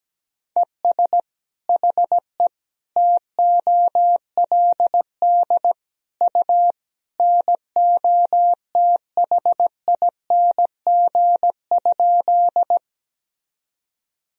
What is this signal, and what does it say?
Morse code, heard as a keyed tone, about 17 wpm, 700 Hz. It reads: ES HE TOLD U NOTHING?
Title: Sentences from Top 500 Words and Abbreviations 17wpm